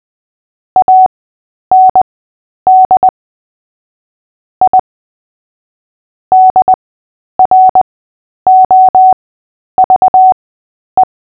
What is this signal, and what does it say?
Morse code, heard as a keyed tone, about 20 wpm, spaced as for 10 wpm, 750 Hz. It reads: AND I DROVE
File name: Short-Story-5-10wpm